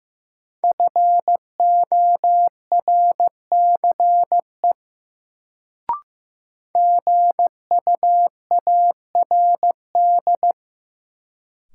Morse code, heard as a keyed tone, about 15 wpm, 700 Hz. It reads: FORCE GUARD
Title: Sentences from Top 700 Words - New Word Drill - 15wpm